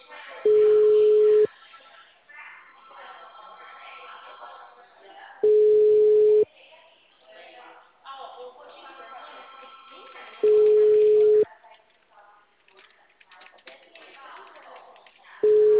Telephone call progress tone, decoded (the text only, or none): ringback tone